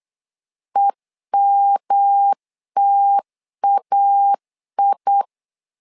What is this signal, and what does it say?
Morse code, heard as a keyed tone, about 8 wpm, 800 Hz. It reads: EMTAI
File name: LEZIONI / 3 MWRO / 3 M 8wpm 2300